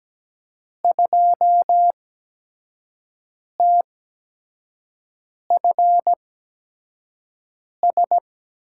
Morse code, heard as a keyed tone, about 17 wpm, spaced as for 5 wpm, 700 Hz. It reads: 2TFS
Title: USA Call Signs - 8x Character Spacing - 17wpm